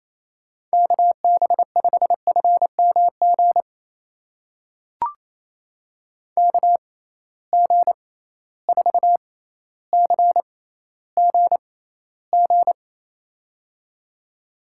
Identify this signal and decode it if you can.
Morse code, heard as a keyed tone, about 28 wpm, spaced as for 14 wpm, 700 Hz. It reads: KB5FMG KG4CGG
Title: USA Call Signs - 6x Character Spacing - 28wpm